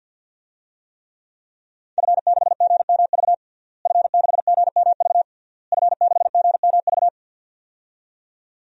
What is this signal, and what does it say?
Morse code, heard as a keyed tone, about 51 wpm, 700 Hz. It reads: F6GMV F6GMV F6GMV
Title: DXCC-100 Call Signs - 5 Characters - Repeated 3x - 50wpm